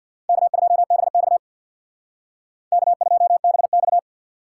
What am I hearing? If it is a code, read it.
Morse code, heard as a keyed tone, about 62 wpm, 700 Hz. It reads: K2BX K2BX
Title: Speed Racing - USA 1x2 Call Signs 50wpm